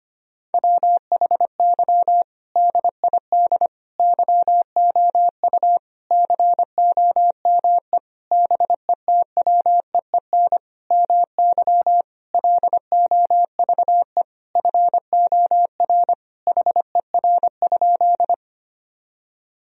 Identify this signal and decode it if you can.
Morse code, heard as a keyed tone, about 25 wpm, 700 Hz. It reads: WHY DID YOU COME BETWEEN MY LOVE FOR HER?
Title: Sentences from Top 1000 Words - Review Part 1 - 25wpm